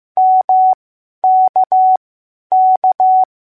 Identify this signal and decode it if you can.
Morse code, heard as a keyed tone, about 15 wpm, spaced as for 10 wpm, 750 Hz. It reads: MKK